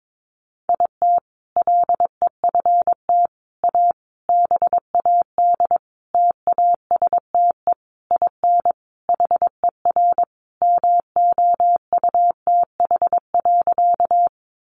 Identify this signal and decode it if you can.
Morse code, heard as a keyed tone, about 22 wpm, 700 Hz. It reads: IT LEFT A BAD TASTE IN HER MOUTH.
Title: Sentences from Top 1500 Words 22wpm